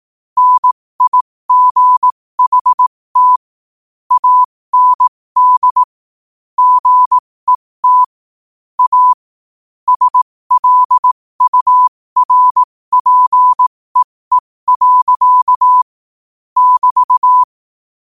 Morse code, heard as a keyed tone, about 18 wpm, spaced as for 15 wpm, 1000 Hz. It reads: NIGHT AND GET A SLURPEE. <BT>
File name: qotdfiles / 15wpm